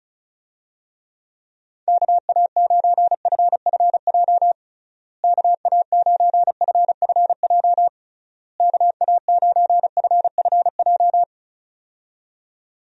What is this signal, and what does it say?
Morse code, heard as a keyed tone, about 34 wpm, 700 Hz. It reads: KA9FFJ KA9FFJ KA9FFJ
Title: USA Call Signs - 6 Characters - Repeated 3x - 35wpm